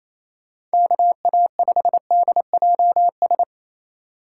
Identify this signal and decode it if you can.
Morse code, heard as a keyed tone, about 35 wpm, 700 Hz. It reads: KA5DJS
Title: Speed Racing - USA 2x3 Call Signs 28wpm